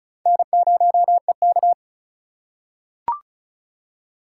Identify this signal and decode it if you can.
Morse code, heard as a keyed tone, about 36 wpm, 700 Hz. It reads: N0EK
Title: USA Call Signs - 4 Characters - 3x Character Spacing - 35wpm